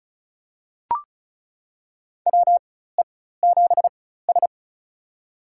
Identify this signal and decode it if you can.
Morse code, heard as a keyed tone, about 36 wpm, spaced as for 16 wpm, 700 Hz. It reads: WE7S
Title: USA Call Signs - 4x Character Spacing - 35wpm